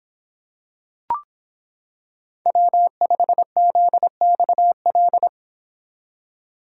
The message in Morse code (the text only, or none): W5ZXL